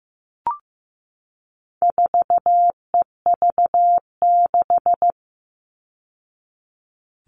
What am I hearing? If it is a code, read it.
Morse code, heard as a keyed tone, about 15 wpm, 700 Hz. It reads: 4EV6
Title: All Cut Numbers - 4-Digit 15wpm